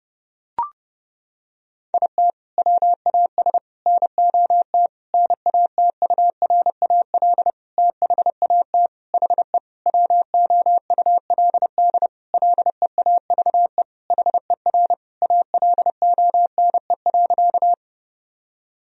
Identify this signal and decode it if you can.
Morse code, heard as a keyed tone, about 30 wpm, 700 Hz. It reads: IT WAS NOT NATURAL THAT HE WOULD LEAVE HER ALONE.
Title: Sentences from Top 1000 Words - Review Part 4 - 30wpm